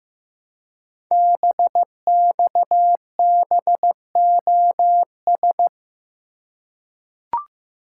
Morse code, heard as a keyed tone, about 15 wpm, 700 Hz. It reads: BXBOS
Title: Code Groups - 5 Letters - 15wpm